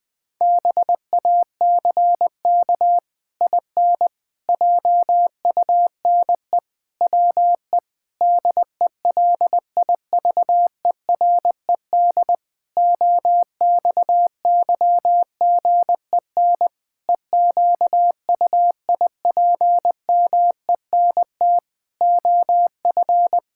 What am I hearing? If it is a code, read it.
Morse code, heard as a keyed tone, about 20 wpm, 700 Hz. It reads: BACK IN JUNE WE DELIVERED OXYGEN EQUIPMENT OF
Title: Pangram Challenge v2 - 20wpm